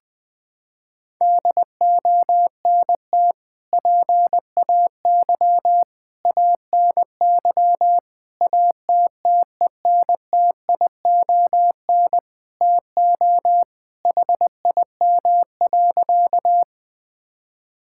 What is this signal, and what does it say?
Morse code, heard as a keyed tone, about 21 wpm, 700 Hz. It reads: DONT PAY ANY ATTENTION TO HIM.
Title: Sentences from Top 1000 Words - Review Part 6 - 20wpm